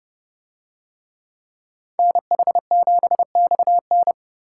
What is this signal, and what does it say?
Morse code, heard as a keyed tone, about 29 wpm, 700 Hz. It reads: NH7XN